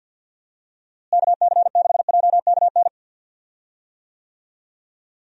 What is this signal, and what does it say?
Morse code, heard as a keyed tone, about 50 wpm, 700 Hz. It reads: KC6JKN
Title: USA Call Signs - 6 Characters - Repeated 2x - 50wpm